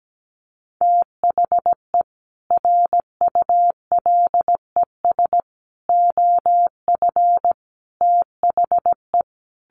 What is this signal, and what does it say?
Morse code, heard as a keyed tone, about 17 wpm, 700 Hz. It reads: THE RULES OF THE
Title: Sets of 5 Words Encore2 17wpm